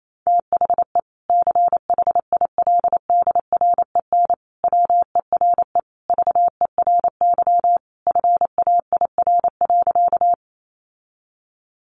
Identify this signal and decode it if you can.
Morse code, heard as a keyed tone, about 28 wpm, 700 Hz. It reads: THE CHILDREN WERE VERY FAIR.
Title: Sentences from Top 1300 Words 28wpm